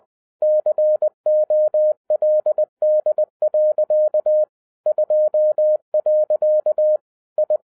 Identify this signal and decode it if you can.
Morse code, heard as a keyed tone, about 20 wpm, 610 Hz. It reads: COLD. 2. I
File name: 2011-10-mp3 / n13-TIPS FROM MY 34 YEARS OF EXPERIENCE 20wpm